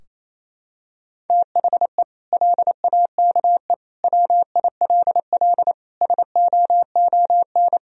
Morse code, heard as a keyed tone, about 28 wpm, 700 Hz. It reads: THE LAKE WILL SOON
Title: Sentences from Top 1000 Words - Review Part 3 - 28wpm